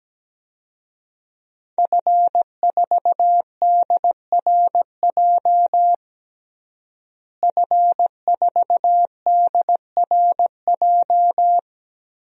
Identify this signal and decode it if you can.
Morse code, heard as a keyed tone, about 17 wpm, 700 Hz. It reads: F4DRJ F4DRJ
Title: DXCC-100 Call Signs - 5 Characters - Repeated 3x - 17wpm